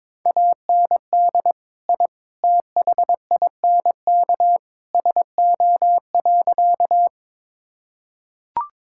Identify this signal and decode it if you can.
Morse code, heard as a keyed tone, about 22 wpm, 700 Hz. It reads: AND I THINK SO.
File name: practice / 22wpm / Sentences with 3-Word Phrases - 22wpm